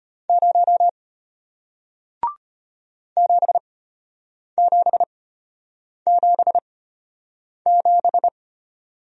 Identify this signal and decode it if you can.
Morse code, heard as a keyed tone, about 33 wpm, spaced as for 8 wpm, 700 Hz. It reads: 0 7777